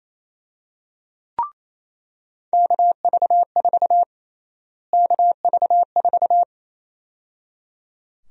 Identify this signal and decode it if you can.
Morse code, heard as a keyed tone, about 28 wpm, 700 Hz. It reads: KV4 KV4
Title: USA Call Sign Prefixes - Repeated 2x - 28wpm